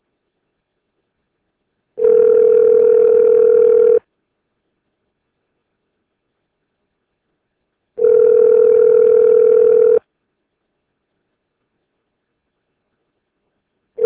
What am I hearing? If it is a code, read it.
Ringback tone (North America), 440 + 480 Hz.